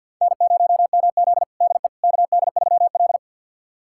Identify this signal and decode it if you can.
Morse code, heard as a keyed tone, about 50 wpm, 700 Hz. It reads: N0MZ DE KD3L
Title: USA Callsign DE Callsign - 4 Characters - 50wpm